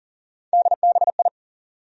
Morse code, heard as a keyed tone, about 40 wpm, 700 Hz. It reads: DBI
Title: USA Call Sign Postfixes - 3 Letters 40wpm